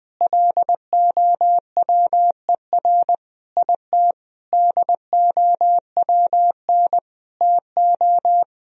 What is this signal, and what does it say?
Morse code, heard as a keyed tone, about 21 wpm, 700 Hz. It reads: LOWER IT DOWN TO
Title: Sentences from Top 1000 Words - Review Part 4 - 20wpm